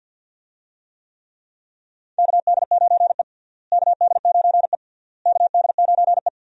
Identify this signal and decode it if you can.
Morse code, heard as a keyed tone, about 53 wpm, 700 Hz. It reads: KD9E KD9E KD9E
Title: USA Call Signs - 4 Characters - Repeated 3x - 50wpm